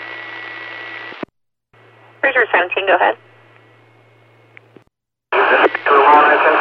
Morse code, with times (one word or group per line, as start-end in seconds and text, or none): none